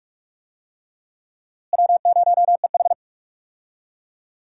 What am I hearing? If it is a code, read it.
Morse code, heard as a keyed tone, about 44 wpm, 700 Hz. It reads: W0EH
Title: USA Call Signs - 4 Characters - Repeated 3x - 45wpm